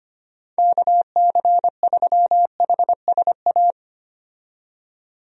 Morse code, heard as a keyed tone, about 25 wpm, 700 Hz. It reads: KC3HSA
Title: USA Call Signs - 6 Characters - 25wpm